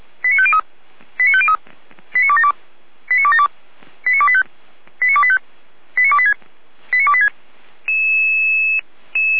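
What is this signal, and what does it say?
Five-tone selective call (ZVEI), heard as quick runs of five tones, tones of about 70 ms. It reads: 89503 89503 89272 89272 89285 89285 89286 89286
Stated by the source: ZVEI five-tone selective call received by SDR, Feuerwehr Orlamünde.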